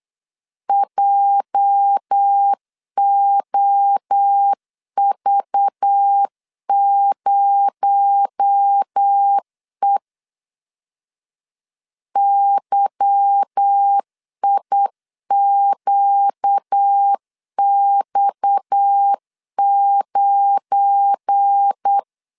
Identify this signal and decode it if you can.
Morse code, heard as a keyed tone, about 8 wpm, 800 Hz. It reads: JOV0E YIQX9
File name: LEZIONI / 8 YJ45 / 8 YJ4 8wpm 3246